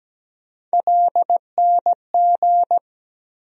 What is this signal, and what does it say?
Morse code, heard as a keyed tone, about 17 wpm, 700 Hz. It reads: LNG
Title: USA Call Sign Postfixes - 3 Letters 17wpm